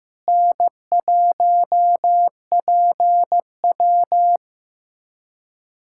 Morse code, heard as a keyed tone, about 15 wpm, 700 Hz. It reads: N1PW
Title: USA Callsign DE Callsign - 4 Characters - 15wpm